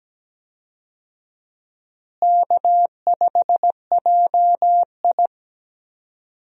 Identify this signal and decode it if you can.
Morse code, heard as a keyed tone, about 17 wpm, 700 Hz. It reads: K5JI
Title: USA Call Signs - 4 Characters - Repeated 3x - 17wpm